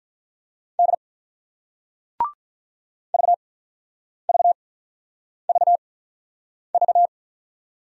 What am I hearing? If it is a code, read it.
Morse code, heard as a keyed tone, about 49 wpm, spaced as for 9 wpm, 700 Hz. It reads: D VVVV